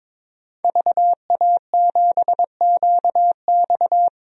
Morse code, heard as a keyed tone, about 22 wpm, 700 Hz. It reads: VA7QX
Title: DXCC-100 Call Signs - 5 Characters - 22wpm